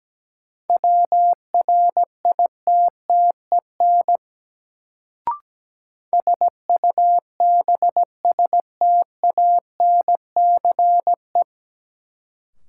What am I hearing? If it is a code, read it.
Morse code, heard as a keyed tone, about 17 wpm, 700 Hz. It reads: WRITTEN SUBSTANCE
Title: Sentences from Top 1500 Words - New Word Drill Alt - 17wpm